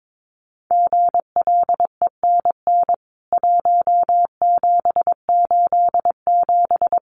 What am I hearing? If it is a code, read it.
Morse code, heard as a keyed tone, about 22 wpm, 700 Hz. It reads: GLENN 1787
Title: CWT Exchange 22wpm